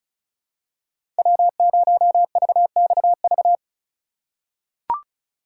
Morse code, heard as a keyed tone, about 36 wpm, 690 Hz. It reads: W0VXV
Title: USA Call Signs - 5 Characters - 3x Character Spacing - 35wpm